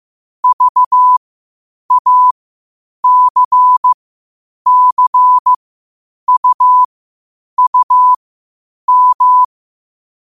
Morse code, heard as a keyed tone, about 15 wpm, spaced as for 8 wpm, 1000 Hz. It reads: VACCUUM
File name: qotdfiles / 8wpm